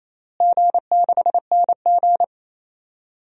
Morse code, heard as a keyed tone, about 28 wpm, 700 Hz. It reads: G6NG
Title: DXCC-100 Call Signs - 4 Characters - Repeated 2x - 28wpm